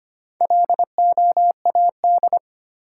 Morse code, heard as a keyed tone, about 25 wpm, 700 Hz. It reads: LOAD
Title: Four Letter Words - 25wpm